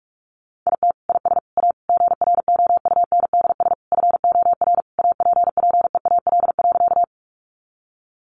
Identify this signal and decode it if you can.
Morse code, heard as a keyed tone, about 44 wpm, 700 Hz. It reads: IT IS A GROUNDS FOR APPEAL.
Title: Sentences from Top 1300 Words 45wpm